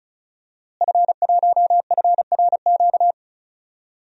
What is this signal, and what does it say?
Morse code, heard as a keyed tone, about 36 wpm, 700 Hz. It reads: F1FRQ
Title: DXCC-100 Call Signs - 5 Characters - Repeated 3x - 35wpm